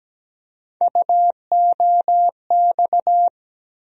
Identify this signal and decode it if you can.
Morse code, heard as a keyed tone, about 17 wpm, 700 Hz. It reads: UOX
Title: USA Call Sign Postfixes - 3 Letters 17wpm